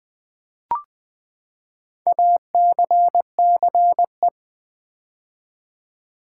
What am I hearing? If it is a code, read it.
Morse code, heard as a keyed tone, about 20 wpm, 700 Hz. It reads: ACCE
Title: Build A Word - 4 Letters At A Time - 10-Letter Words - 20wpm